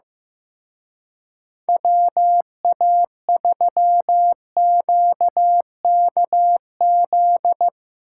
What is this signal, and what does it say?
Morse code, heard as a keyed tone, about 15 wpm, 700 Hz. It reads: WA3QKZ
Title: Speed Racing - USA 2x3 Call Signs 15wpm